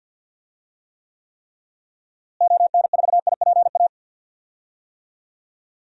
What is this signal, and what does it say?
Morse code, heard as a keyed tone, about 51 wpm, 700 Hz. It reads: ON4IPA
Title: DXCC-100 Call Signs - 6 Characters - Repeated 3x - 50wpm